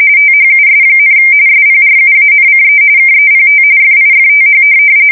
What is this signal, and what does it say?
RTTY, about 45.45 baud. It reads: THUS, SPRUANCE KEPT FIFTH
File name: fsk170 / USS-New-Jersey